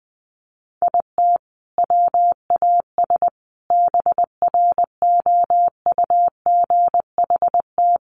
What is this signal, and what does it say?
Morse code, heard as a keyed tone, about 20 wpm, 700 Hz. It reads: IT WAS BROUGHT
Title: Sentences from Top 500 Words 20wpm